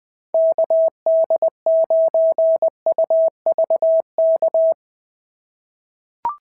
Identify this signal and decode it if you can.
Morse code, heard as a keyed tone, about 19 wpm, 690 Hz. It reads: KD9UVK
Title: POTA QSO Practice - 20wpm